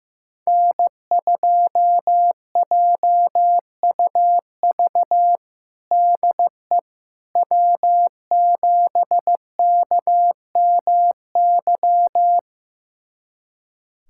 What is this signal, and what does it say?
Morse code, heard as a keyed tone, about 15 wpm, 700 Hz. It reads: N2JUV DE W7KMY
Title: USA Callsign DE Callsign - 5 Characters - 15wpm